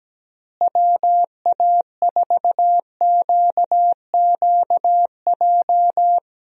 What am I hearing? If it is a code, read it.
Morse code, heard as a keyed tone, about 17 wpm, 700 Hz. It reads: WA4QQJ
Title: USA Call Signs - 6 Characters - Repeated 2x - 17wpm